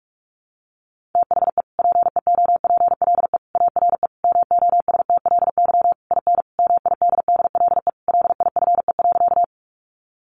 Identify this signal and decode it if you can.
Morse code, heard as a keyed tone, about 44 wpm, 700 Hz. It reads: THE PEOPLE ARE MOSTLY IN MIDDLE LIFE.
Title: Sentences from Top 1300 Words 45wpm